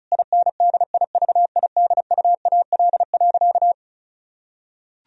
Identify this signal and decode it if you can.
Morse code, heard as a keyed tone, about 34 wpm, 700 Hz. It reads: INDIVIDUAL.
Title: Sentences from Top 1000 Words - Review Part 4 - 35wpm